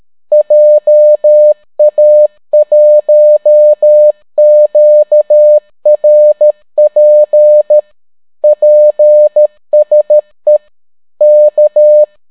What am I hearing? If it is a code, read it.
Morse code, heard as a keyed tone, about 13 wpm, 600 Hz. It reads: JA1QRP PSE K